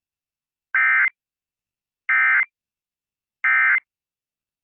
Emergency Alert System (SAME) tones, heard as harsh buzzing bursts. It is heard from 0.7 to 3.8 s.